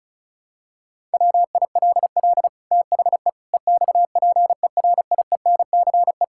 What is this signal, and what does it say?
Morse code, heard as a keyed tone, about 36 wpm, 700 Hz. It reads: WILL THE EXPERIENCE
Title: Sentences from Top 1000 Words - Review Part 5 - 35wpm